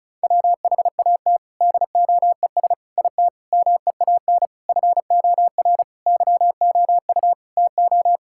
Morse code, heard as a keyed tone, about 36 wpm, 700 Hz. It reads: WHAT DOES IT MEAN FOR YOU TO
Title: Sentences from Top 1000 Words - Review Part 6 - 35wpm